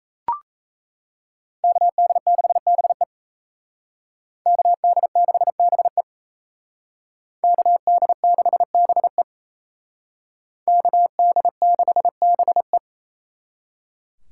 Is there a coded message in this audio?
Yes: Morse code from 1.6 to 12.8 s, heard as a keyed tone, about 35 wpm, 700 Hz. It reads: KD6BE KD6BE KD6BE KD6BE